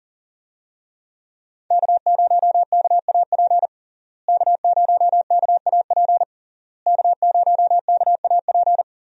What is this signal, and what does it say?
Morse code, heard as a keyed tone, about 40 wpm, 700 Hz. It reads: K0KAP K0KAP K0KAP
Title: USA Call Signs - 5 Characters - Repeated 3x - 40wpm